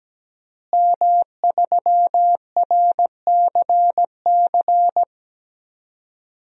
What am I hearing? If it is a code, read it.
Morse code, heard as a keyed tone, about 17 wpm, 700 Hz. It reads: M3RCC
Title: DXCC-100 Call Signs - 5 Characters - Repeated 2x - 17wpm